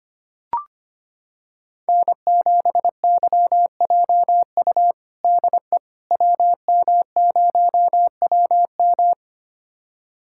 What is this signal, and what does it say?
Morse code, heard as a keyed tone, about 25 wpm, 700 Hz. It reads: N7YJU DE WM0WM
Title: USA Callsign DE Callsign - 5 Characters - 25wpm